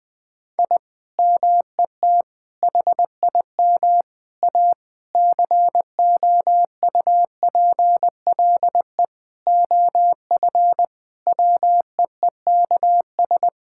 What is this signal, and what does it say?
Morse code, heard as a keyed tone, about 20 wpm, 700 Hz. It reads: I MET HIM A COUPLE OF WEEKS
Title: Sentences with 5-Word Phrases - 20wpm